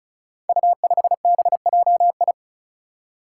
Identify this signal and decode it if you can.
Morse code, heard as a keyed tone, about 36 wpm, 700 Hz. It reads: U5BJI
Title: Five Challenging Letter-Number Combinations 35wpm